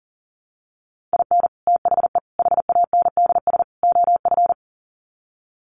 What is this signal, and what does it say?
Morse code, heard as a keyed tone, about 40 wpm, 700 Hz. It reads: IN THE HANDS OF
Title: Sets of 4 Words 40wpm